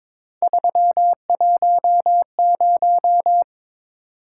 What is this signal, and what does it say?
Morse code, heard as a keyed tone, about 22 wpm, 700 Hz. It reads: 310
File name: practice / 22wpm / Three Letter-Number Combinations - 22wpm